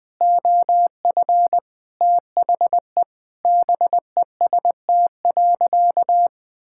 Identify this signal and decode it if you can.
Morse code, heard as a keyed tone, about 20 wpm, 700 Hz. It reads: OF THE BEST.
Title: Sentences with 4-Word Phrases - 20wpm